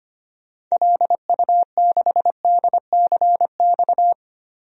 Morse code, heard as a keyed tone, about 25 wpm, 700 Hz. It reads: LU6DCX
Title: DXCC-100 Call Signs - 6 Characters - 25wpm